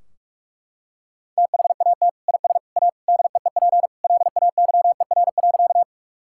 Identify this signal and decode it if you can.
Morse code, heard as a keyed tone, about 42 wpm, 700 Hz. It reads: THAT IS A DEEP LAYER.
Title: Sentences from Top 1600 Words 45wpm